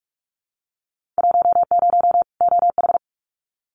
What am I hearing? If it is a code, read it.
Morse code, heard as a keyed tone, about 46 wpm, 700 Hz. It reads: 10 OH
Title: Winter Field Day Exchanges 45wpm